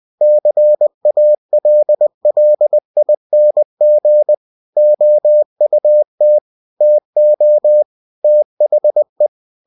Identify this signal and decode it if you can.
Morse code, heard as a keyed tone, about 20 wpm, 600 Hz. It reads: CALLING OUT TO THE
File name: winnie-20wpm